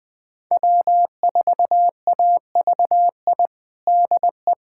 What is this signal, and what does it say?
Morse code, heard as a keyed tone, about 21 wpm, 700 Hz. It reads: W4AVI DE
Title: USA Callsign DE Callsign - 5 Characters - 20wpm